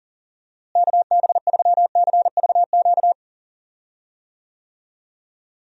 Morse code, heard as a keyed tone, about 40 wpm, 700 Hz. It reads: KB3CVQ
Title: USA Call Signs - 6 Characters - Repeated 3x - 40wpm